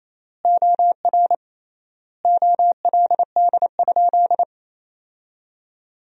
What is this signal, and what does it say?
Morse code, heard as a keyed tone, about 28 wpm, 700 Hz. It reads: OR OLD?
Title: 3x Word-Spacing - Sentences from Top 400 Words - 28wpm